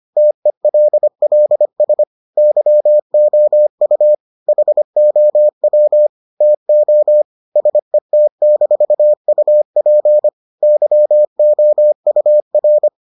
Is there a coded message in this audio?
Yes: Morse code from 0.2 to 12.9 s, heard as a keyed tone, about 25 wpm, 600 Hz. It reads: TELLS YOU HOW TO SET-UP YOUR